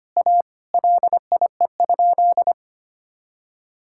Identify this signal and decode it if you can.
Morse code, heard as a keyed tone, about 25 wpm, 700 Hz. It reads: A LIE?
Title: Sentences from Top 1600 Words 25wpm